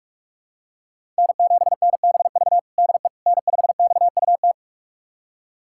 Morse code, heard as a keyed tone, about 46 wpm, 700 Hz. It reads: N7NBV DE N5XUT
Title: USA Callsign DE Callsign - 5 Characters - 45wpm